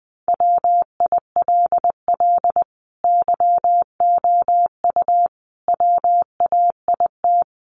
Morse code, heard as a keyed tone, about 20 wpm, 700 Hz. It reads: WILL YOU WAIT
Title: Sentences from Top 900 Words 20wpm